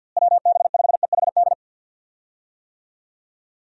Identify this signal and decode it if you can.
Morse code, heard as a keyed tone, about 49 wpm, 700 Hz. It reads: WB5EHD
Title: USA Call Signs - 6 Characters - Repeated 2x - 50wpm